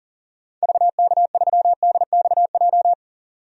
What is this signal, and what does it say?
Morse code, heard as a keyed tone, about 40 wpm, 700 Hz. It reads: VK3DXJ
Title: DXCC-100 Call Signs - 6 Characters - 40wpm